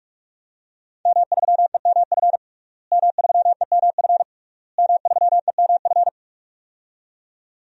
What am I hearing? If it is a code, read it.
Morse code, heard as a keyed tone, about 46 wpm, 700 Hz. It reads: M3EMF M3EMF M3EMF
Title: DXCC-100 Call Signs - 5 Characters - Repeated 3x - 45wpm